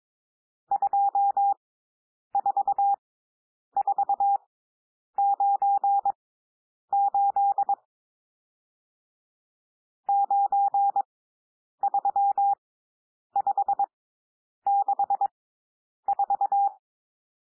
Morse code, heard as a keyed tone, about 22 wpm, spaced as for 9 wpm, 800 Hz. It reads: 24498 93564
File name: cwmp3 / N1 SOUND NUMBERS 22 wpm